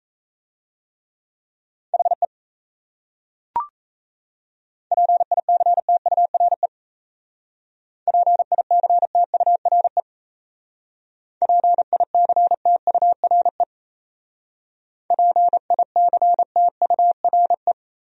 Morse code, heard as a keyed tone, about 37 wpm, 700 Hz. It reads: HE PICTURE PICTURE PICTURE PICTURE